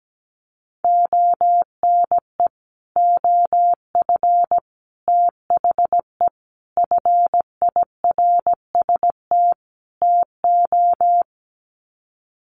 Morse code, heard as a keyed tone, about 17 wpm, 700 Hz. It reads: ONE OF THE FIRST TO
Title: Sets of 5 Words Encore 17wpm